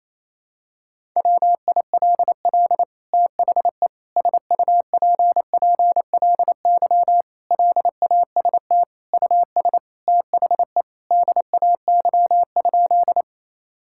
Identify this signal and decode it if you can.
Morse code, heard as a keyed tone, about 28 wpm, 700 Hz. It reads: WILL THE SUPPLY LAST US THE DAY?
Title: Sentences from Top 1000 Words - Review Part 5 - 28wpm